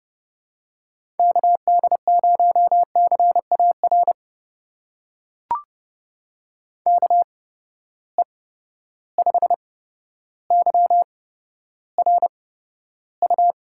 Morse code, heard as a keyed tone, about 30 wpm, spaced as for 13 wpm, 700 Hz. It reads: KD0CAR KE5YRU
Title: USA Call Signs - 6 Characters - 8x Character Spacing - 30wpm